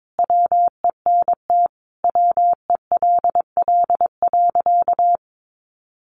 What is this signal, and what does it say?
Morse code, heard as a keyed tone, about 22 wpm, 700 Hz. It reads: WENT WELL.
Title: Sentences from Top 800 Words 22wpm